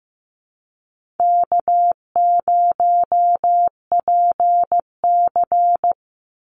Morse code, heard as a keyed tone, about 15 wpm, 700 Hz. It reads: K0PC
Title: Easy Call Signs - 15wpm